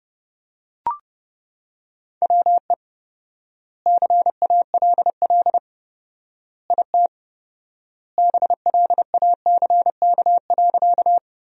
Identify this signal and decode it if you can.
Morse code, heard as a keyed tone, about 30 wpm, 700 Hz. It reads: WE CALL IT BLACK.
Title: 4x Word-Spacing - Sentences from Top 400 Words - 30wpm